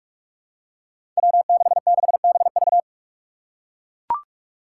Morse code, heard as a keyed tone, about 42 wpm, 700 Hz. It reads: W66BV
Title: Five Challenging Letter-Number Combinations 45wpm